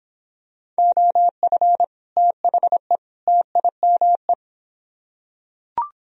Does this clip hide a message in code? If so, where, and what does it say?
Yes: Morse code from 0.8 to 4.3 s, heard as a keyed tone, about 26 wpm, 700 Hz. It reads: OF THE TIME